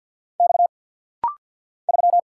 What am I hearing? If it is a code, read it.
Morse code, heard as a keyed tone, about 50 wpm, 700 Hz. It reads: X 3
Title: Single Letter-Number - Rapid-Fire 50wpm